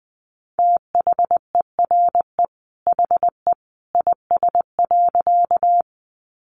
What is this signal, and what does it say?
Morse code, heard as a keyed tone, about 20 wpm, 700 Hz. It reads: THERE HE IS.
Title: A Sound Of Thunder 20wpm